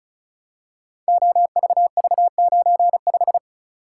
Morse code, heard as a keyed tone, about 36 wpm, 700 Hz. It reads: OVV95
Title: Code Groups - 5 Letter:Numbers - 35wpm